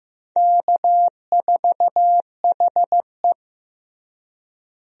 Morse code, heard as a keyed tone, about 15 wpm, 700 Hz. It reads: K4HE
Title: USA Call Signs - 4 Characters - Repeated 2x - 15wpm